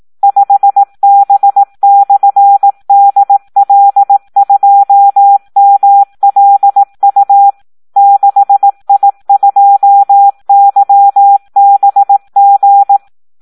Morse code, heard as a keyed tone, about 18 wpm, 800 Hz. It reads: 5B/DL2MLU 6I2YBG
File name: Audio practica CW / SD 1 18